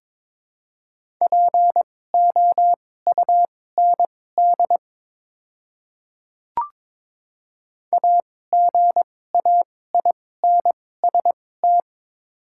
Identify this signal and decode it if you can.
Morse code, heard as a keyed tone, about 22 wpm, spaced as for 16 wpm, 700 Hz. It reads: POUND AGAINST
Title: 2x Character Spacing - Top 500 Words - 22wpm